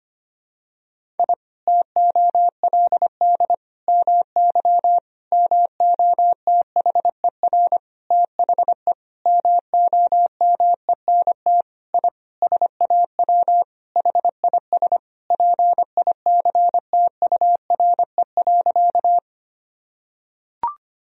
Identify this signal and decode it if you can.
Morse code, heard as a keyed tone, about 25 wpm, 700 Hz. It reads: I TOLD MY MOTHER THE MOMENT I SAW HIS PICTURE.
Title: Sentences from Top 1000 Words - Review Part 1 - 25wpm